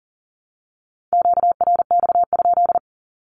Morse code, heard as a keyed tone, about 40 wpm, 700 Hz. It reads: QRX?
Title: Q-Codes 40wpm